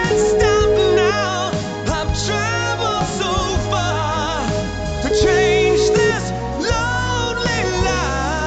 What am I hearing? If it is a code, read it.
Ringback tone (Europe, 425 Hz), with noise.